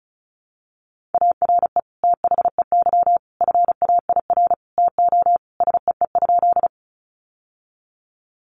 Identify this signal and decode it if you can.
Morse code, heard as a keyed tone, about 34 wpm, 700 Hz. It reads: ARE THEY FAIR TO SEE?
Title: Sentences from Top 1300 Words 35wpm